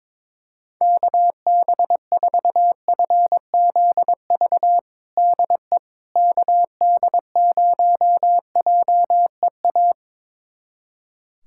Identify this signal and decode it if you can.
Morse code, heard as a keyed tone, about 22 wpm, 700 Hz. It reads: KB4FZV DE KD0JEA
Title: USA Callsign DE Callsign - 6 Characters - 22wpm